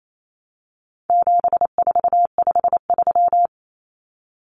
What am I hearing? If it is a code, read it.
Morse code, heard as a keyed tone, about 28 wpm, 700 Hz. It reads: 7453
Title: Numbers - 4-Digit 28wpm